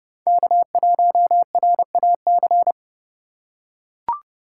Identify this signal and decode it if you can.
Morse code, heard as a keyed tone, about 30 wpm, 700 Hz. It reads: K1RAC
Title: USA Call Signs - 5 Characters - 2x Character Spacing - 30wpm